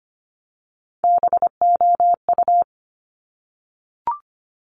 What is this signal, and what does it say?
Morse code, heard as a keyed tone, about 25 wpm, 700 Hz. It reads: BOU
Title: Top Three Letter Combinations in Words 25wpm